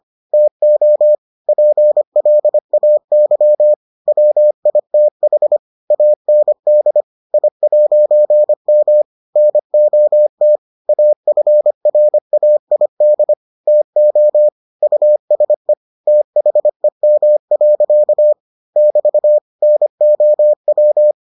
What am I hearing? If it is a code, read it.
Morse code, heard as a keyed tone, about 25 wpm, 600 Hz. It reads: TO PLAY WITH AND I'M NOT AFRAID TO USE THEM. <BT> NOW